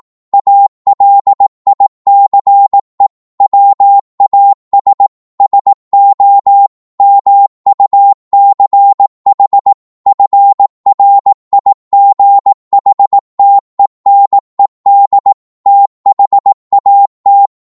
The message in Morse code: ALICE WAS SO MUCH FRIGHTENED THAT